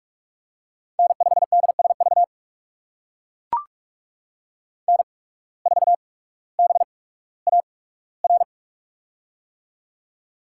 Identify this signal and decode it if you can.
Morse code, heard as a keyed tone, about 44 wpm, spaced as for 19 wpm, 700 Hz. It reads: N5DSV N4BAR
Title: USA Call Signs - 5 Characters - 8x Character Spacing - 45wpm